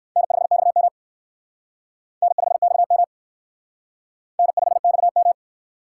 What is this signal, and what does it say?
Morse code, heard as a keyed tone, about 60 wpm, 700 Hz. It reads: N5XM N5XM N5XM